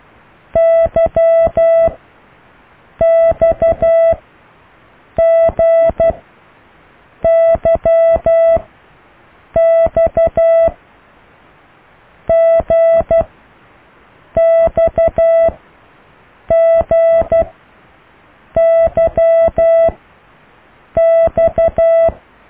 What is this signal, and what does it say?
Morse code, heard as a keyed tone, about 12 wpm, 660 Hz. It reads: Y X G Y X G X G Y X